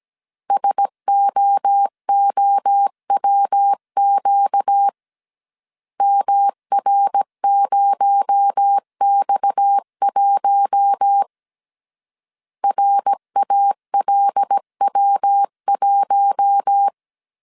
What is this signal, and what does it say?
Morse code, heard as a keyed tone, about 17 wpm, 800 Hz. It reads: SOOWQ MR0X1 RALW1
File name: 16wpm